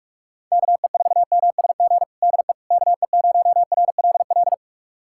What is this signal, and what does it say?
Morse code, heard as a keyed tone, about 43 wpm, 700 Hz. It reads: KE4MSG DE KE0RLL
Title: USA Callsign DE Callsign - 6 Characters - 45wpm